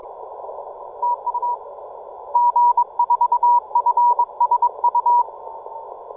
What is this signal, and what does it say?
Morse code, heard as a keyed tone, about 24 wpm, 970 Hz. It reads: TU G4FSU